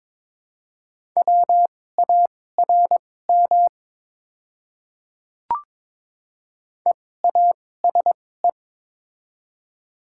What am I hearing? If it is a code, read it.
Morse code, heard as a keyed tone, about 22 wpm, spaced as for 16 wpm, 700 Hz. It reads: WARM EASE
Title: 2x Character Spacing - Top 500 Words - 22wpm